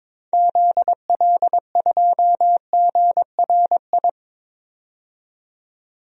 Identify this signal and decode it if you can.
Morse code, heard as a keyed tone, about 21 wpm, 700 Hz. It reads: ZL2GRI
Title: DXCC-100 Call Signs - 6 Characters - 22wpm